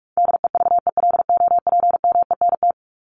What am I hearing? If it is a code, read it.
Morse code, heard as a keyed tone, about 44 wpm, 700 Hz. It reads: DEVELOPMENT
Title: Top 501-750 Words 45wpm